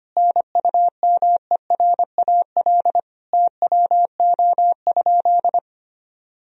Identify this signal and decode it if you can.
Morse code, heard as a keyed tone, about 25 wpm, 700 Hz. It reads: NUMERAL TWO?
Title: Sentences from Top 1000 Words - Review Part 2 - 25wpm